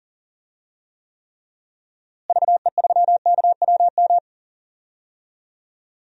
Morse code, heard as a keyed tone, about 40 wpm, 700 Hz. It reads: VE3KWM
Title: DXCC-100 Call Signs - 6 Characters - Repeated 3x - 40wpm